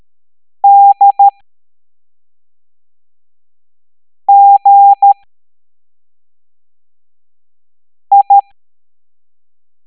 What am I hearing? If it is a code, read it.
Morse code, heard as a keyed tone, about 13 wpm, 800 Hz. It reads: D G I
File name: hamdata / L03-EX01-IGC